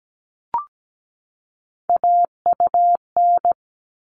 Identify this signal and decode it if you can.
Morse code, heard as a keyed tone, about 17 wpm, 700 Hz. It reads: AUN
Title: All Cut Numbers - 3-Digit 17wpm